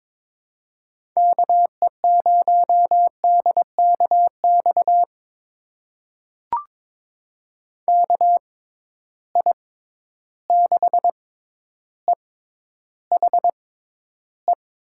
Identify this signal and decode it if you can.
Morse code, heard as a keyed tone, about 21 wpm, spaced as for 11 wpm, 700 Hz. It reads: KE0DKX KI6EHE
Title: USA Call Signs - 6 Characters - 6x Character Spacing - 22wpm